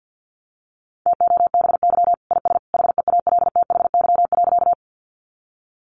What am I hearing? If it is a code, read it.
Morse code, heard as a keyed tone, about 50 wpm, 700 Hz. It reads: TOBY IS HEALTHY.
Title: Easy Phrases 50wpm